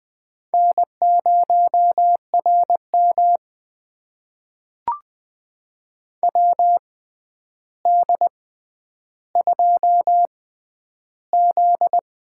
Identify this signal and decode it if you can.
Morse code, heard as a keyed tone, about 20 wpm, spaced as for 10 wpm, 700 Hz. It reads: N0RM WD2Z
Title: USA Call Signs - 4 Characters - 6x Character Spacing - 20wpm